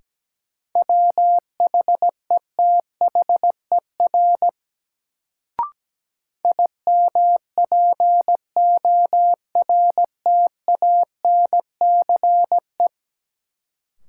Morse code, heard as a keyed tone, about 17 wpm, 700 Hz. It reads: WHETHER IMPORTANCE
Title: Sets of 3 Words - New Word Drill - 17wpm